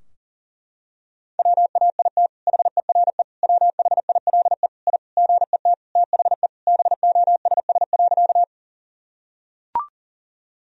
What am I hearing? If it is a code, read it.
Morse code, heard as a keyed tone, about 40 wpm, 700 Hz. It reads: WAIT HERE WHILE I GET THE BOSS.
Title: Sentences from Top 1000 Words - Review Part 4 - 40wpm